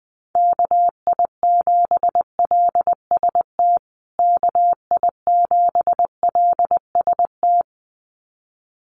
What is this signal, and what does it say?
Morse code, heard as a keyed tone, about 20 wpm, 700 Hz. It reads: KI7LST KI7LST
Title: USA Call Signs Repeated 2x 20wpm